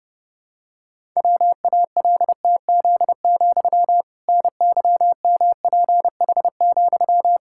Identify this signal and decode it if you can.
Morse code, heard as a keyed tone, about 32 wpm, 700 Hz. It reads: WALTZ, NYMPH,
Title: Pangram Challenge v2 - 30wpm